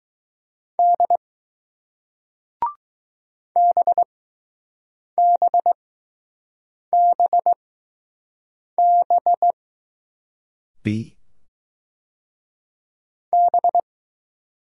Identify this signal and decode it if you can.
Morse code, heard as a keyed tone, about 20 wpm, spaced as for 6 wpm, 700 Hz. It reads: D BBBB B